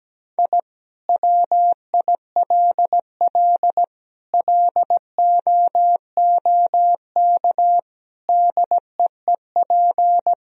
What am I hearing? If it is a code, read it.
Morse code, heard as a keyed tone, about 17 wpm, 700 Hz. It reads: I WILL LOOK DEEP
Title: Sentences from Top 1000 Words - Review Part 3 - 17wpm